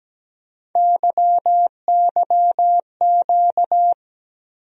Morse code, heard as a keyed tone, about 17 wpm, 700 Hz. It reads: YYQ
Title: USA Call Sign Postfixes - 3 Letters 17wpm